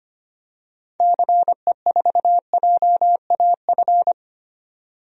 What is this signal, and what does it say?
Morse code, heard as a keyed tone, about 25 wpm, 700 Hz. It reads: CE4JAF
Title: DXCC-100 Call Signs - 6 Characters - 25wpm